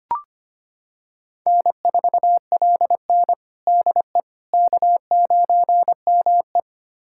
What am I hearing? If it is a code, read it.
Morse code, heard as a keyed tone, about 25 wpm, 700 Hz. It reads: N4LN DE K9ME
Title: USA Callsign DE Callsign - 4 Characters - 25wpm